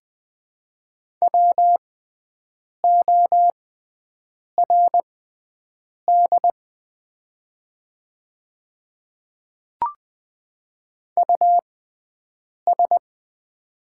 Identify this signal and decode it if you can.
Morse code, heard as a keyed tone, about 20 wpm, spaced as for 7 wpm, 700 Hz. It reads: WORD US
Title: 6x Character Spacing - Top 300 Words - 20wpm